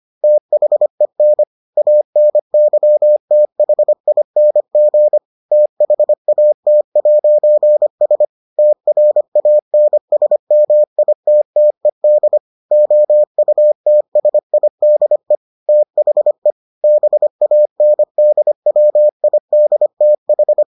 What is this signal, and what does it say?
Morse code, heard as a keyed tone, about 25 wpm, 600 Hz. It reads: THEN ANYTHING THAT'S TRANSMITTED OUTSIDE THE BANDWIDTH